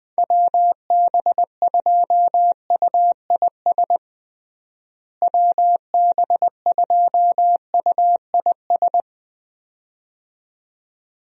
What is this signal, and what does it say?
Morse code, heard as a keyed tone, about 20 wpm, 700 Hz. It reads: WB2UIS WB2UIS
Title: USA Call Signs - 6 Characters - Repeated 2x - 20wpm